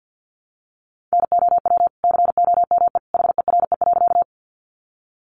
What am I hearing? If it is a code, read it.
Morse code, heard as a keyed tone, about 50 wpm, 700 Hz. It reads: NOW COME HERE.
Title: Sentences from Top 200 Words 50wpm